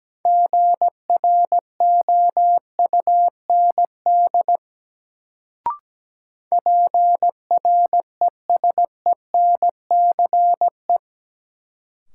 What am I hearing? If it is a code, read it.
Morse code, heard as a keyed tone, about 17 wpm, 700 Hz. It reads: GROUND PRESENCE
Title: Sets of 3 Words - New Word Drill - 17wpm